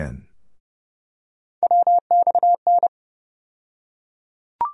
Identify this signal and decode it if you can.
Morse code, heard as a keyed tone, about 30 wpm, 700 Hz. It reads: WXN
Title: USA Call Sign Postfixes - 3 Letters 30wpm